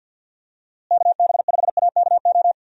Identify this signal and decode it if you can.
Morse code, heard as a keyed tone, about 47 wpm, 700 Hz. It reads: KB5AKO
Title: USA Call Signs - 6 Characters - Repeated 2x - 50wpm